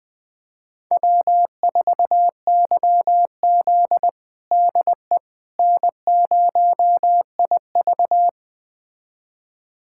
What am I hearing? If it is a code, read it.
Morse code, heard as a keyed tone, about 21 wpm, 700 Hz. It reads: W4YZ DE N0IV
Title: USA Callsign DE Callsign - 4 Characters - 20wpm